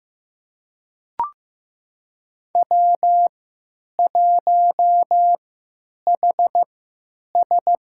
Morse code, heard as a keyed tone, about 15 wpm, spaced as for 9 wpm, 700 Hz. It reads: W1HS
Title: USA Call Signs - 3x Character Spacing - 15wpm